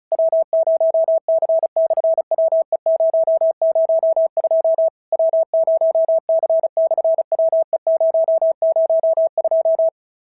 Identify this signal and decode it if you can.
Morse code, heard as a keyed tone, about 36 wpm, 650 Hz. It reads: W0C/WE002 W0C/WE002
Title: SOTA QSO Practice 35wpm